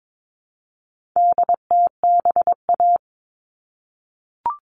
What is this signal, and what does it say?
Morse code, heard as a keyed tone, about 22 wpm, 700 Hz. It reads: DTBA
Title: All Cut Numbers - 4-Digit 22wpm